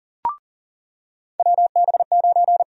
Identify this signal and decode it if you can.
Morse code, heard as a keyed tone, about 40 wpm, 700 Hz. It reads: WB9
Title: USA Call Sign Prefixes - 40wpm